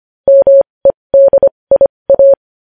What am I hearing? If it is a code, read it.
Morse code, heard as a keyed tone, about 25 wpm, spaced as for 20 wpm, 550 Hz. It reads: MEDIA
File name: podcast / News headlines-1x-25x20